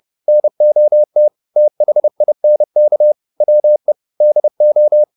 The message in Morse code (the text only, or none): NOT THINK WE DO